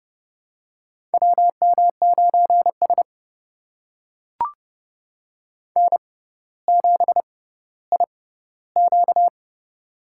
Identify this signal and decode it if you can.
Morse code, heard as a keyed tone, about 32 wpm, spaced as for 16 wpm, 700 Hz. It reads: WM9S N7IQ
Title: USA Call Signs - 4 Characters - 6x Character Spacing - 30wpm